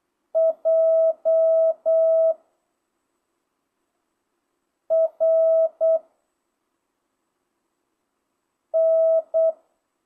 Morse code, heard as a keyed tone, about 8 wpm, 660 Hz. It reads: J R N
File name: jh8cbh / 2025cw